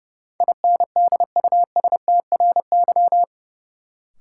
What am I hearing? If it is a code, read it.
Morse code, heard as a keyed tone, about 32 wpm, 700 Hz. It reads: INDUSTRY
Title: Sentences from Top 1000 Words - Review Part 4 - New Word Drill - 30wpm